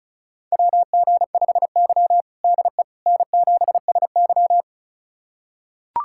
Morse code, heard as a keyed tone, about 36 wpm, 700 Hz. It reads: WG5Y DE N7SY
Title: USA Callsign DE Callsign - 4 Characters - 35wpm